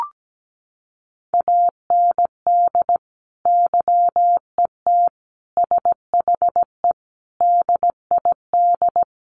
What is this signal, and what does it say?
Morse code, heard as a keyed tone, about 17 wpm, 700 Hz. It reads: AND YET SHE DID
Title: Sentences from Top 1500 Words 17wpm